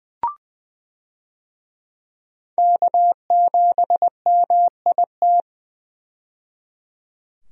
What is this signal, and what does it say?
Morse code, heard as a keyed tone, about 20 wpm, 700 Hz. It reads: K7MIT
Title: USA Call Signs - 5 Characters - 20wpm